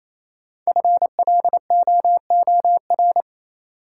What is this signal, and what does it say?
Morse code, heard as a keyed tone, about 28 wpm, 700 Hz. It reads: FLOOR
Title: Sets of 3 Words - New Word Drill - 28wpm